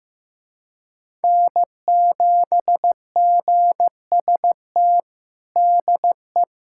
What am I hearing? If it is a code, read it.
Morse code, heard as a keyed tone, about 15 wpm, 700 Hz. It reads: N7GST DE
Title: USA Callsign DE Callsign - 5 Characters - 15wpm